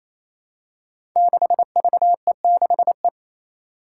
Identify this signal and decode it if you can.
Morse code, heard as a keyed tone, about 28 wpm, 700 Hz. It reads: 6VE6E
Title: Five Challenging Letter-Number Combinations 28wpm